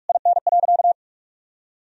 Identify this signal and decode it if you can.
Morse code, heard as a keyed tone, about 46 wpm, 700 Hz. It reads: IN.
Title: Sentences from Top 1000 Words - Review Part 2 - 45wpm